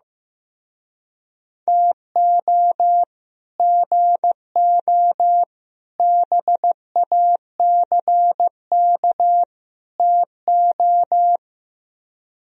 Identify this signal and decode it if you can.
Morse code, heard as a keyed tone, about 15 wpm, 700 Hz. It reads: TO GO BACK TO
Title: Speed-Racing - Sets of 4 Words 15wpm